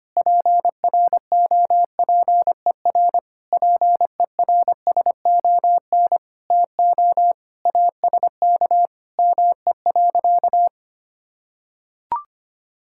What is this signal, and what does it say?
Morse code, heard as a keyed tone, about 25 wpm, 700 Hz. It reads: PROPER PERSON TO ASK ME.
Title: Sentences from Top 1000 Words - Review Part 5 - 25wpm